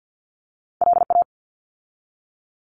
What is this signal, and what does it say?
Morse code, heard as a keyed tone, about 48 wpm, 700 Hz. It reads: LA
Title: ARRL Sections 50wpm